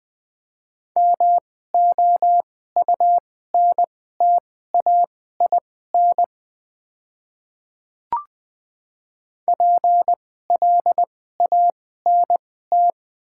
Morse code, heard as a keyed tone, about 20 wpm, spaced as for 15 wpm, 700 Hz. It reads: MOUNTAIN PLANT
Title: 2x Character Spacing - Top 300 Words - 20wpm